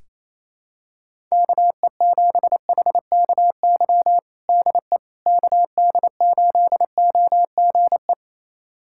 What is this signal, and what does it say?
Morse code, heard as a keyed tone, about 28 wpm, 700 Hz. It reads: KE7HKY DE KD8OGE
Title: USA Callsign DE Callsign - 6 Characters - 28wpm